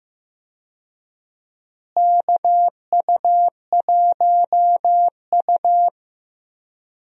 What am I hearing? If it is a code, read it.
Morse code, heard as a keyed tone, about 15 wpm, 700 Hz. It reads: KU1U
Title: USA Call Signs - 4 Characters - Repeated 2x - 15wpm